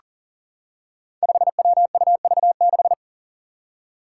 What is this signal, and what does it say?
Morse code, heard as a keyed tone, about 39 wpm, 700 Hz. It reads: 5WUV6
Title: Five Challenging Letter-Number Combinations 40wpm